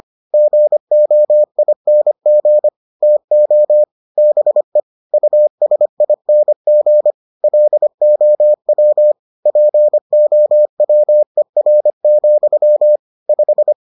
Morse code, heard as a keyed tone, about 25 wpm, 600 Hz. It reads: GOING TO BE USING LOW POWER, 5